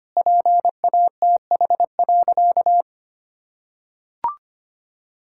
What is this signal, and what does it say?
Morse code, heard as a keyed tone, about 25 wpm, 700 Hz. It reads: PATH.
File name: practice / 25wpm / Sentences with 2-Word Phrases - 25wpm